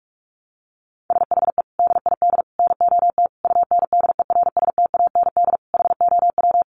Morse code, heard as a keyed tone, about 44 wpm, 700 Hz. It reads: SHE DID NOT UNDERSTAND HOW
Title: Sentences from Top 600 Words 45wpm